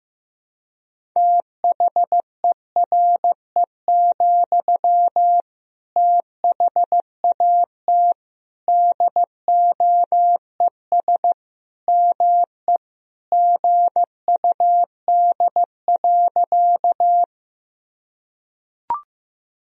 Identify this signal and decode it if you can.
Morse code, heard as a keyed tone, about 15 wpm, 700 Hz. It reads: THERE, THAT DOES ME GUD.
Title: Sentences from Top 200 Words and Abbreviations 15wpm